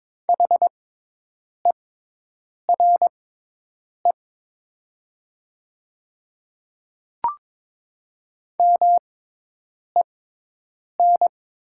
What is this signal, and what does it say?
Morse code, heard as a keyed tone, about 22 wpm, spaced as for 8 wpm, 700 Hz. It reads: HERE MEN
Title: 6x Character Spacing - Top 200 Words - 22wpm